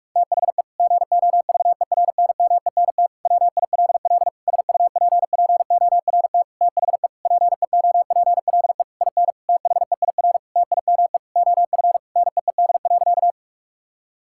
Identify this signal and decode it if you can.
Morse code, heard as a keyed tone, about 44 wpm, 700 Hz. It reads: THE GOVERNMENT WILL SUPPORT THE PEOPLE IN THEIR TIME OF NEED.
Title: Sentences from Top 1000 Words - Review Part 5 - 45wpm